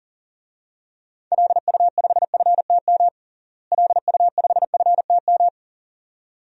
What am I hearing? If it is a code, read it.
Morse code, heard as a keyed tone, about 40 wpm, 700 Hz. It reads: LU5FTM LU5FTM
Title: DXCC-100 Call Signs - 6 Characters - Repeated 2x - 40wpm